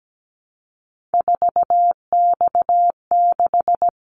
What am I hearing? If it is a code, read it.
Morse code, heard as a keyed tone, about 17 wpm, 700 Hz. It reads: 4X6
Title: Callsign Prefixes for Top 50 DXCC - 17wpm